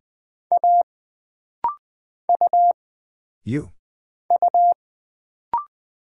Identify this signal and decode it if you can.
Morse code, heard as a keyed tone, about 20 wpm, 700 Hz. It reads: A U U
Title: Single Character (Letter:Number:Punctuation:Prosign) - Rapid-Fire - 20wpm